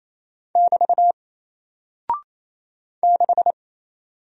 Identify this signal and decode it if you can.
Morse code, heard as a keyed tone, about 28 wpm, 700 Hz. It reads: <BT> 6
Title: Single Character (Letter:Number:Punctuation:Prosign) Alt 28wpm